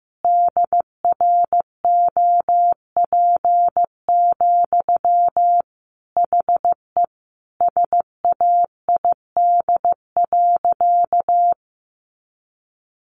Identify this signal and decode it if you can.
Morse code, heard as a keyed tone, about 14 wpm, 700 Hz. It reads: DROP, HE SAID.
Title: Sentences from Top 1400 Words 15wpm